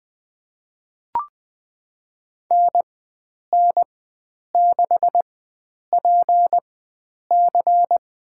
Morse code, heard as a keyed tone, about 20 wpm, spaced as for 9 wpm, 700 Hz. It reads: NN6PC
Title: USA Call Signs - 5 Characters - 4x Character Spacing - 20wpm